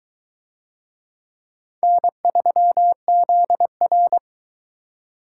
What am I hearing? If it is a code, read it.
Morse code, heard as a keyed tone, about 22 wpm, 700 Hz. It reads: N3ZR